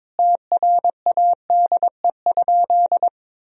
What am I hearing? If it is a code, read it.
Morse code, heard as a keyed tone, about 22 wpm, 700 Hz. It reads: TRADE?
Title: Sentences from Top 1000 Words - Review Part 4 - 22wpm